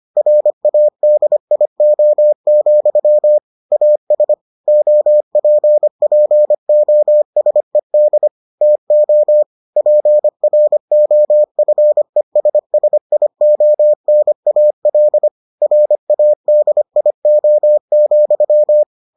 Morse code, heard as a keyed tone, about 25 wpm, 600 Hz. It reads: RADIO, AS OPPOSED TO PROFESSIONAL RADIO,